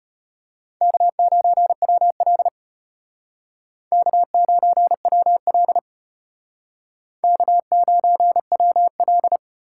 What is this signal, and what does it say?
Morse code, heard as a keyed tone, about 33 wpm, 700 Hz. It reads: K9WL K9WL K9WL